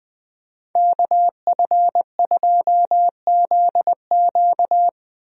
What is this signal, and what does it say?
Morse code, heard as a keyed tone, about 21 wpm, 700 Hz. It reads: KF2ZQ
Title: USA Call Signs - 5 Characters - Repeated 3x - 20wpm